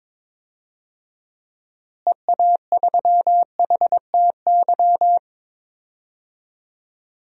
Morse code, heard as a keyed tone, about 22 wpm, 700 Hz. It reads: EA3HTY
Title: DXCC-100 Call Signs - 6 Characters - Repeated 2x - 22wpm